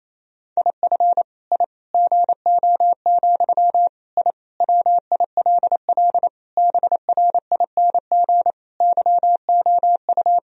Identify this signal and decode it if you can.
Morse code, heard as a keyed tone, about 28 wpm, 700 Hz. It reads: IF I GO, I WILL BRING YOU
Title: Sentences from Top 1000 Words - Review Part 3 - 28wpm